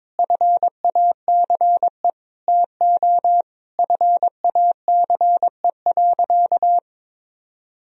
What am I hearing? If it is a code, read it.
Morse code, heard as a keyed tone, about 22 wpm, 700 Hz. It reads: FACE TO FACE.
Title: Sentences from Top 1000 Words - Review Part 2 - 22wpm